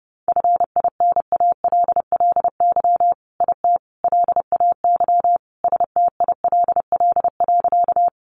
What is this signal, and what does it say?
Morse code, heard as a keyed tone, about 30 wpm, 700 Hz. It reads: FINALLY IT LAY STILL.
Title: Sentences from Top 1000 Words 30wpm